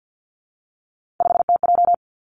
Morse code, heard as a keyed tone, about 50 wpm, 700 Hz. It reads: 5TJ
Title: Three Challenging Letter-Number Combinations 50wpm